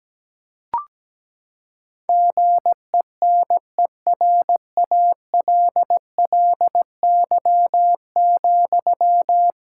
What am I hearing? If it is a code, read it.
Morse code, heard as a keyed tone, about 17 wpm, 700 Hz. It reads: GENERALLY,
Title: Sentences from Top 1000 Words - Review Part 6 - 17wpm